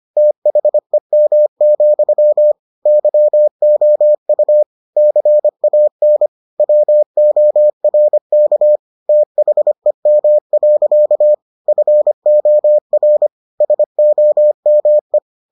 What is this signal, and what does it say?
Morse code, heard as a keyed tone, about 25 wpm, 600 Hz. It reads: THEM, YOU CAN WORK THEM. FOR SOME